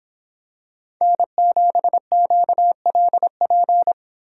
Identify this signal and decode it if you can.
Morse code, heard as a keyed tone, about 26 wpm, 700 Hz. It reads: N7QLP
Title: Speed Racing - USA 1x3 Call Signs 22wpm